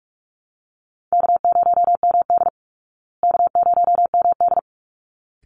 Morse code, heard as a keyed tone, about 44 wpm, 700 Hz. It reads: K0MD K0MD
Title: Call Signs Repeated 3x - 45wpm